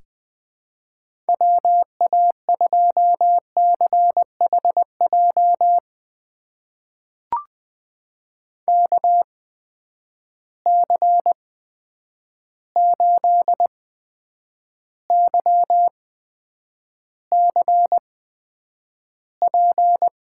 Morse code, heard as a keyed tone, about 20 wpm, spaced as for 9 wpm, 700 Hz. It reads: WA2CHJ KC8YCP
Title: USA Call Signs - 6 Characters - 8x Character Spacing - 20wpm